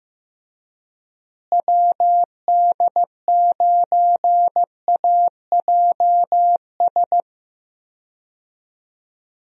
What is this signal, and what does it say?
Morse code, heard as a keyed tone, about 15 wpm, 700 Hz. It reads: WD9AJS
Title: USA Call Signs - 6 Characters - Repeated 3x - 15wpm